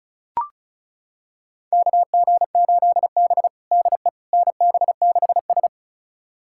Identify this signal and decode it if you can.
Morse code, heard as a keyed tone, about 36 wpm, 690 Hz. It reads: KG8B DE NB6S
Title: USA Callsign DE Callsign - 4 Characters - 35wpm